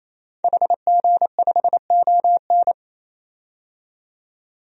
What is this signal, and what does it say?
Morse code, heard as a keyed tone, about 28 wpm, 700 Hz. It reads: HG5ON
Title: DXCC-100 Call Signs - 5 Characters - 28wpm